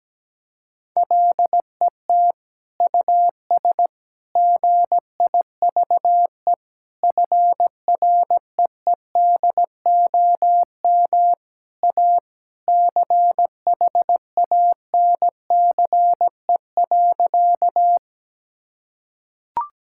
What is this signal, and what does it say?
Morse code, heard as a keyed tone, about 17 wpm, 700 Hz. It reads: LET US GIVE FREEDOM A CHANCE.
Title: Sentences from Top 1000 Words - Review Part 5 - 17wpm